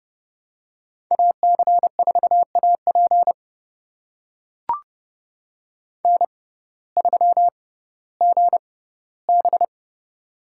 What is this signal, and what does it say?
Morse code, heard as a keyed tone, about 30 wpm, spaced as for 17 wpm, 700 Hz. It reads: AC4AP N3GB
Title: USA Call Signs - 6x Character Spacing - 30wpm